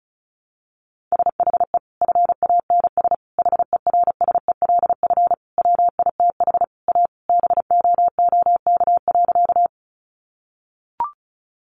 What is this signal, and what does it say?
Morse code, heard as a keyed tone, about 34 wpm, 700 Hz. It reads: SHE FANS HERSELF WITH A BOOK.
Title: Easy Phrases 35wpm